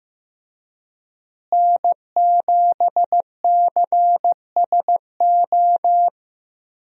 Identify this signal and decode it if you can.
Morse code, heard as a keyed tone, about 15 wpm, 700 Hz. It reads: N7CSO
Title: USA Call Signs - 5 Characters - Repeated 3x - 15wpm